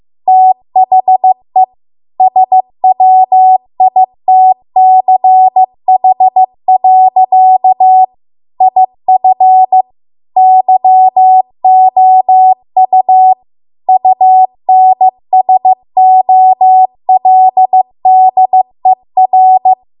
Morse code, heard as a keyed tone, about 15 wpm, 750 Hz. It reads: THE SWITCH. IF YOU UNSOLDER
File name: Archive / 15 WPM / 15WPM